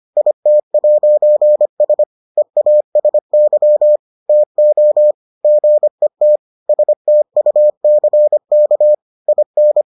Morse code, heard as a keyed tone, about 25 wpm, 600 Hz. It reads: IT'S EASY TO GET STUCK IN